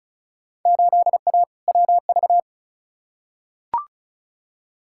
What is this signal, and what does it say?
Morse code, heard as a keyed tone, about 36 wpm, 700 Hz. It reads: 8A WV
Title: Field Day Exchanges - 35wpm